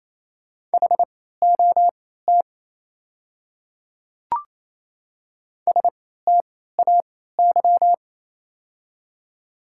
Morse code, heard as a keyed tone, about 28 wpm, spaced as for 16 wpm, 700 Hz. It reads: HOT STAY
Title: 3x Character Spacing - Top 500 Words - 28wpm